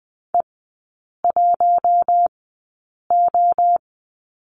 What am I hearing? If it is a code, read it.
Morse code, heard as a keyed tone, about 20 wpm, spaced as for 8 wpm, 700 Hz. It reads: E1O
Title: 053 Lesson 10e - Letter r - Three Characters 20wpm